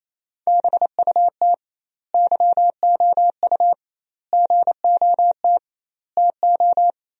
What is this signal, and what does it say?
Morse code, heard as a keyed tone, about 27 wpm, 700 Hz. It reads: BUT YOU GOT TO
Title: 2x Word-Spacing - Sentences from Top 300 Words - 28wpm